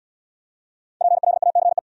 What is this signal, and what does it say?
Morse code, heard as a keyed tone, about 64 wpm, 700 Hz. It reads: PLACE